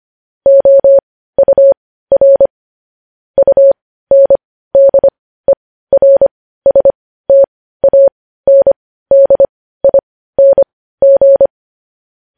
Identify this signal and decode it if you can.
Morse code, heard as a keyed tone, about 25 wpm, spaced as for 15 wpm, 550 Hz. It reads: OUR UNDERSTANDING
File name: podcast / News headlines-1x-25x15